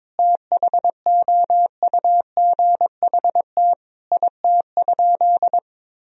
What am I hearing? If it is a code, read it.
Morse code, heard as a keyed tone, about 21 wpm, 700 Hz. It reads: THOUGHT IT?
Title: Sentences from Top 300 Words and Abbreviations 22wpm